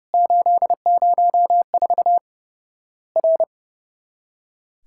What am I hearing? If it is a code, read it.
Morse code, heard as a keyed tone, about 30 wpm, 700 Hz. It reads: 804 R
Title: Worked All Europe DX Contest - QTC Exchange - 30wpm